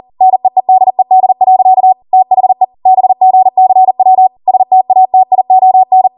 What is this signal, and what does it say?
Morse code, heard as a keyed tone, about 40 wpm, 750 Hz. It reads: NEEDED. THE BGCW STATION